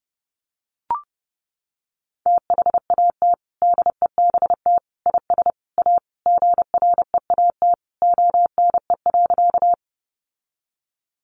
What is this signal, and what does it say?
Morse code, heard as a keyed tone, about 30 wpm, 700 Hz. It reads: THAT DEBT IS A GREAT ONE.
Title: Sentences from Top 1300 Words 30wpm